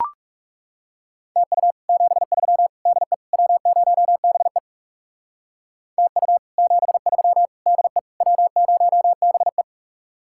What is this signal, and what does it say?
Morse code, heard as a keyed tone, about 38 wpm, 700 Hz. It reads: TU 73 DE W0BE TU 73 DE W0BE